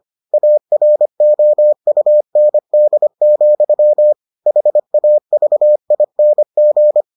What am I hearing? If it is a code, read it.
Morse code, heard as a keyed tone, about 25 wpm, 600 Hz. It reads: AROUND, HAVING